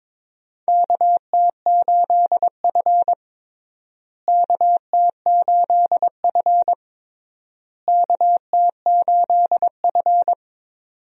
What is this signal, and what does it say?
Morse code, heard as a keyed tone, about 23 wpm, 700 Hz. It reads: KT8F KT8F KT8F
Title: USA Call Signs - 4 Characters - Repeated 3x - 22wpm